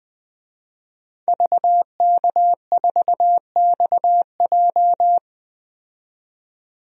Morse code, heard as a keyed tone, about 20 wpm, 700 Hz. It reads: VK4XJ
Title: DXCC-100 Call Signs - 5 Characters - 20wpm